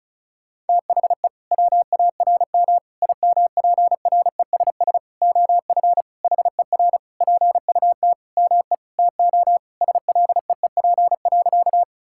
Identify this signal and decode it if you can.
Morse code, heard as a keyed tone, about 36 wpm, 700 Hz. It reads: THE WARM IMPRESS OF HER PUT ME TO SLEEP.
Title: Sentences from Top 1600 Words 35wpm